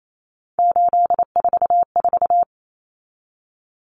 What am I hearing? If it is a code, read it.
Morse code, heard as a keyed tone, about 28 wpm, 700 Hz. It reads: 844
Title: Common Cut Numbers - 3-Digit 28wpm